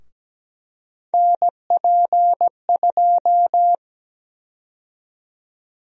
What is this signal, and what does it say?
Morse code, heard as a keyed tone, about 17 wpm, 700 Hz. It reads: NP2
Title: USA Call Sign Prefixes - 17wpm